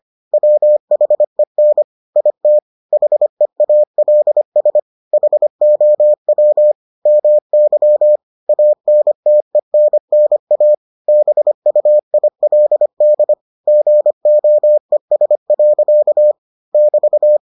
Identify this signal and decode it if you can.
Morse code, heard as a keyed tone, about 25 wpm, 600 Hz. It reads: WHEN IT HEALS HOW MY ANTENNA BUILD GOES. <BT>